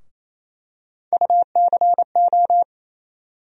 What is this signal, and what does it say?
Morse code, heard as a keyed tone, about 28 wpm, 700 Hz. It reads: UCO
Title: USA Call Sign Postfixes - 3 Letters 28wpm